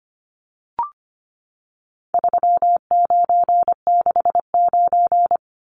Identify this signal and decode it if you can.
Morse code, heard as a keyed tone, about 25 wpm, 700 Hz. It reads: 3969
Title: Numbers - 4-Digit 25wpm